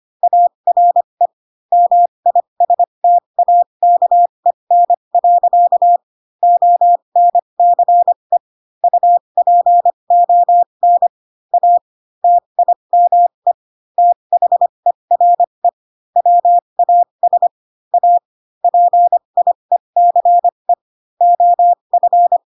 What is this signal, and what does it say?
Morse code, heard as a keyed tone, about 25 wpm, spaced as for 22 wpm, 700 Hz. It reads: ARE MISTAKEN. ONCE UPON A TIME THERE WAS A PIECE OF